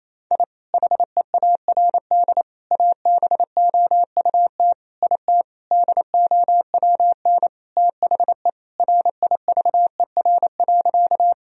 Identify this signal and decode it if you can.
Morse code, heard as a keyed tone, about 28 wpm, 700 Hz. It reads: I HEARD ABOUT IT DOWN THE RIVER.
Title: Sentences from Top 1000 Words - Review Part 1 - 28wpm